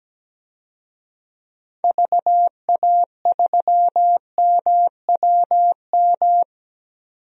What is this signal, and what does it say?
Morse code, heard as a keyed tone, about 17 wpm, 700 Hz. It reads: VA3MWM
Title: DXCC-100 Call Signs - 6 Characters - 17wpm